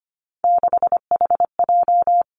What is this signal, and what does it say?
Morse code, heard as a keyed tone, about 25 wpm, 700 Hz. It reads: 6HJ
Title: Three Challenging Letter-Number Combinations 25wpm